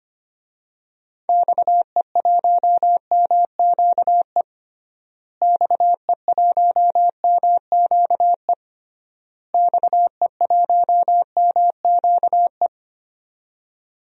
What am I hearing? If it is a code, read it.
Morse code, heard as a keyed tone, about 25 wpm, 700 Hz. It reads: XE1MQE XE1MQE XE1MQE
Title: DXCC-100 Call Signs - 6 Characters - Repeated 3x - 25wpm